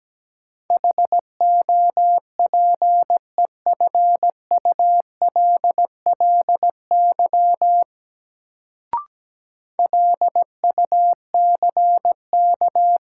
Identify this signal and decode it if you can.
Morse code, heard as a keyed tone, about 17 wpm, 700 Hz. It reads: HOPEFULLY LUCK
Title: Sentences from Top 1500 Words - New Word Drill - 17wpm